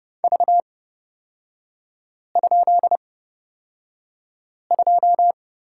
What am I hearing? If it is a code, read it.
Morse code, heard as a keyed tone, about 29 wpm, spaced as for 5 wpm, 700 Hz. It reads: V?2
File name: Lesson 38e - Letter q - Three Characters 30wpm